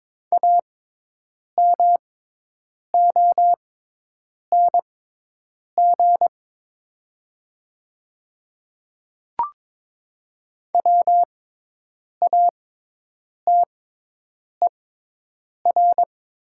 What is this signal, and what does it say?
Morse code, heard as a keyed tone, about 22 wpm, spaced as for 8 wpm, 700 Hz. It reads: AMONG WATER
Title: 6x Character Spacing - Top 500 Words - 22wpm